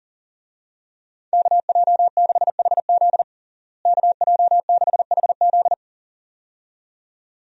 Morse code, heard as a keyed tone, about 40 wpm, 700 Hz. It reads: KJ6HZ KJ6HZ
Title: USA Call Signs - 5 Characters - Repeated 2x - 40wpm